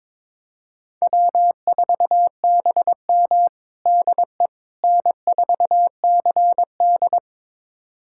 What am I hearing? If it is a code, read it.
Morse code, heard as a keyed tone, about 21 wpm, 700 Hz. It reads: W4BM DE N4CD
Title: USA Callsign DE Callsign - 4 Characters - 22wpm